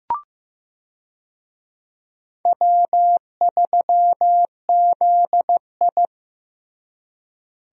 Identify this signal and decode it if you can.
Morse code, heard as a keyed tone, about 15 wpm, 700 Hz. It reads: W3ZI
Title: USA Call Signs - 4 Characters - Repeated 2x - 15wpm